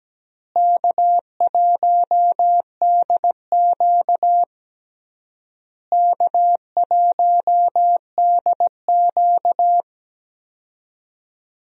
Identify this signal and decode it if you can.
Morse code, heard as a keyed tone, about 17 wpm, 700 Hz. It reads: K1DQ K1DQ
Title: USA Call Signs - 4 Characters - Repeated 3x - 17wpm